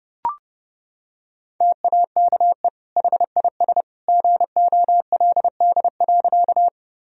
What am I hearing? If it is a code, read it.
Morse code, heard as a keyed tone, about 32 wpm, 700 Hz. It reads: TAKE HIS GOLD.
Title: Sentences from Top 1000 Words - Review Part 3 - 30wpm